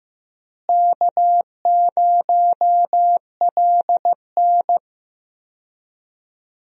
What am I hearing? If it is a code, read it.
Morse code, heard as a keyed tone, about 14 wpm, 700 Hz. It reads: K0LN
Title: USA Call Signs - 4 Characters - Repeated 3x - 15wpm